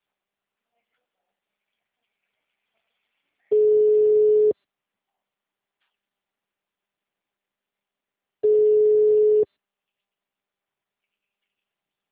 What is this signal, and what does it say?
Ringback tone (Europe, 425 Hz).
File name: out- -2031- - - .25500